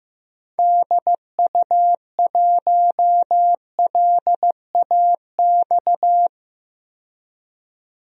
Morse code, heard as a keyed tone, about 15 wpm, 700 Hz. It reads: DU1LAX
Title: DXCC-100 Call Signs - 6 Characters - 15wpm